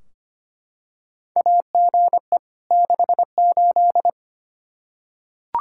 Morse code, heard as a keyed tone, about 25 wpm, 700 Hz. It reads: AGE 68
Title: Practice QSO - Exchange 3 - More Info - 25wpm